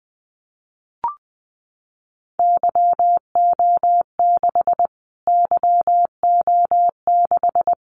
Morse code, heard as a keyed tone, about 20 wpm, 700 Hz. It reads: YO6 YO6
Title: Callsign Prefixes for Top 25 DXCC - Repeated 3x - 20wpm